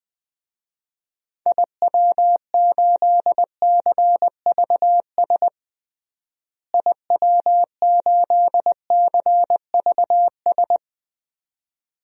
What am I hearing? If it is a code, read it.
Morse code, heard as a keyed tone, about 20 wpm, 700 Hz. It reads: IW8CVS IW8CVS
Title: DXCC-100 Call Signs - 6 Characters - Repeated 2x - 20wpm